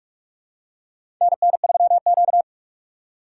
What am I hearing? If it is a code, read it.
Morse code, heard as a keyed tone, about 46 wpm, 700 Hz. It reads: NN3Q
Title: USA Call Signs - 4 Characters - 2x Character Spacing - 45wpm